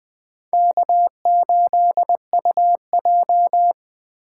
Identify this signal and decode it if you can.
Morse code, heard as a keyed tone, about 21 wpm, 700 Hz. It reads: K8UJ